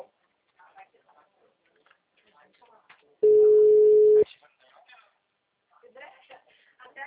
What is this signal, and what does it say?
Ringback tone (Europe, 425 Hz), with speech.